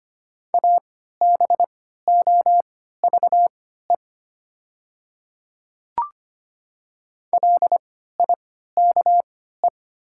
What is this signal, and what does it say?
Morse code, heard as a keyed tone, about 25 wpm, spaced as for 14 wpm, 700 Hz. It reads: ABOVE LIKE
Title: 3x Character Spacing - Top 500 Words - 25wpm